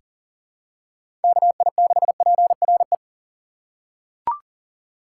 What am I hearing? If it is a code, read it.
Morse code, heard as a keyed tone, about 40 wpm, 700 Hz. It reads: KI6PRE
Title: USA Call Signs - 6 Characters - 2x Character Spacing - 40wpm